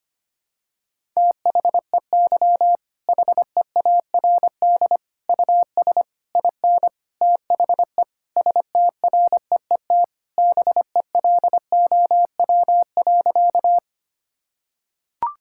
Morse code, heard as a keyed tone, about 25 wpm, 700 Hz. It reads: THEY HEARD US IN THE STREET BELOW.
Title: Sentences from Top 1000 Words - Review Part 5 - 25wpm